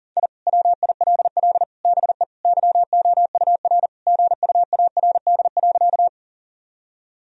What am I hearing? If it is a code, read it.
Morse code, heard as a keyed tone, about 37 wpm, 700 Hz. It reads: I WILL BE YOUR GUARD.
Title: Sentences from Top 1000 Words - Review Part 4 - 40wpm